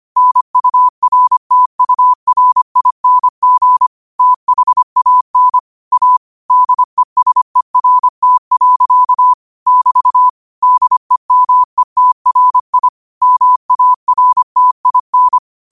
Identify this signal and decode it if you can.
Morse code, heard as a keyed tone, about 25 wpm, 1000 Hz. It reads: NURTURING THAN A DESERT. <BT> DEMETRI MARTIN
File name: qotdfiles / 25wpm